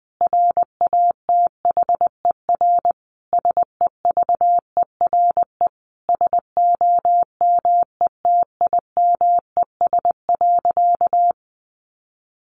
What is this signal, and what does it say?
Morse code, heard as a keyed tone, about 20 wpm, 700 Hz. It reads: RATHER SEVERE SOMETIMES.
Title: Sentences from Top 1400 Words 20wpm